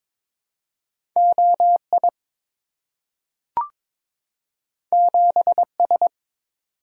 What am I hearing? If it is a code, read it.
Morse code, heard as a keyed tone, about 23 wpm, 700 Hz. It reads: OI 7S
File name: practice / 22wpm / Two Letter-Number Combinations - 22wpm